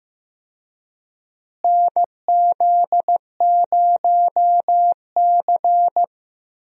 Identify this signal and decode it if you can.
Morse code, heard as a keyed tone, about 16 wpm, 700 Hz. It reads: NZ0C
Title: USA Call Signs - 4 Characters - Repeated 2x - 15wpm